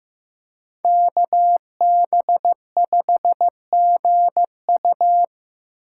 Morse code, heard as a keyed tone, about 15 wpm, 700 Hz. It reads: KB5GU
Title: USA Call Signs - 6x Character Spacing - 15wpm